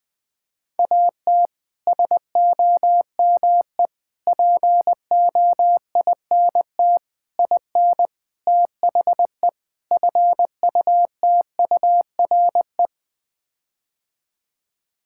Morse code, heard as a keyed tone, about 20 wpm, 700 Hz. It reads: AT SOME POINT IN THE FUTURE
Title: Sets of 6 Words - 20wpm